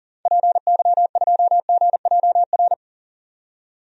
Morse code, heard as a keyed tone, about 40 wpm, 700 Hz. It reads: PY2GJR
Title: DXCC-100 Call Signs - 6 Characters - 40wpm